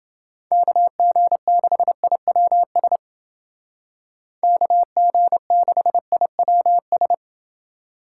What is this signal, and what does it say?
Morse code, heard as a keyed tone, about 30 wpm, 700 Hz. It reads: KG6IWS KG6IWS